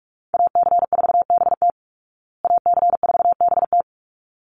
Morse code, heard as a keyed tone, about 44 wpm, 700 Hz. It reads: AC4BT AC4BT
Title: Call Signs Repeated 3x - 45wpm